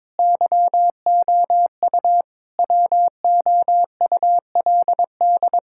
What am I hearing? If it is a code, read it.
Morse code, heard as a keyed tone, about 22 wpm, 700 Hz. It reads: YOU WOULD
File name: Sets of 9 Words - 22wpm